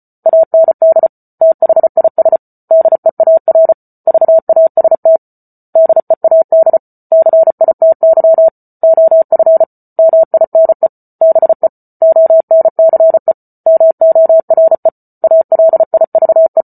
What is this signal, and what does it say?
Morse code, heard as a keyed tone, about 34 wpm, 650 Hz. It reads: AND THIS DEAR VAST DEAD CITY OF MINE BE ONCE MORE ALIVE